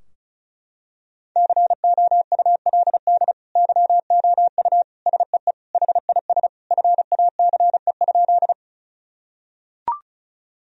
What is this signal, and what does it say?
Morse code, heard as a keyed tone, about 36 wpm, 700 Hz. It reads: COULD YOU SEE HIS FACE?
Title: Sentences from Top 1000 Words - Review Part 2 - 35wpm